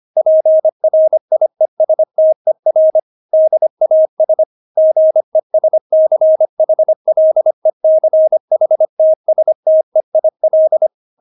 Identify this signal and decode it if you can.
Morse code, heard as a keyed tone, about 25 wpm, 630 Hz. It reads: PRIESTER DAS GESCHLECHTSTEIL
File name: Witz6 25wpm